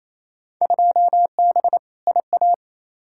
Morse code, heard as a keyed tone, about 28 wpm, 700 Hz. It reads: 2B IA
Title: Field Day Exchanges Encore - 28wpm